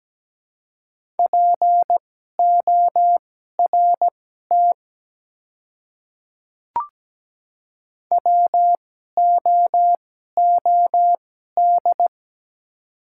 Morse code, heard as a keyed tone, about 17 wpm, spaced as for 12 wpm, 700 Hz. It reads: PORT WOOD
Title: 2x Character Spacing - Top 300 Words - 17wpm